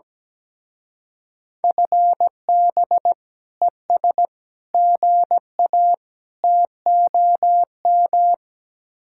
Morse code, heard as a keyed tone, about 18 wpm, 700 Hz. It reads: FB ES GA TOM
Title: Speed-Racing Common QSO Phrases - 17wpm